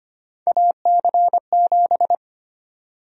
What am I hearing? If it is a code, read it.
Morse code, heard as a keyed tone, about 25 wpm, 700 Hz. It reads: AC7
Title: USA Call Sign Prefixes - 25wpm